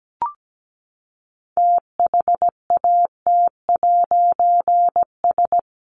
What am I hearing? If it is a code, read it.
Morse code, heard as a keyed tone, about 17 wpm, 700 Hz. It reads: THAT'S
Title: A Sound Of Thunder 17wpm